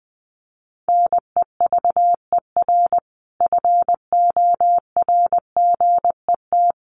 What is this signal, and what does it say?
Morse code, heard as a keyed tone, about 20 wpm, 700 Hz. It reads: NEVER FORGET
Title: Sentences from Top 1200 Words 20wpm